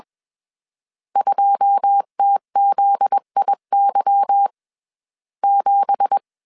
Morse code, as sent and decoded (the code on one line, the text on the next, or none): ..--- - --.. .. -.-- / --...
2TZIY 7